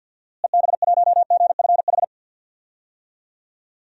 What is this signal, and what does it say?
Morse code, heard as a keyed tone, about 46 wpm, 700 Hz. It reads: EB1GFH
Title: DXCC-100 Call Signs - 6 Characters - Repeated 3x - 50wpm